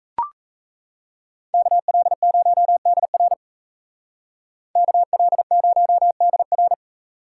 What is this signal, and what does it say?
Morse code, heard as a keyed tone, about 33 wpm, 700 Hz. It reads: KL0DR KL0DR
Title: Speed Racing - USA 2x2 Call Signs 28wpm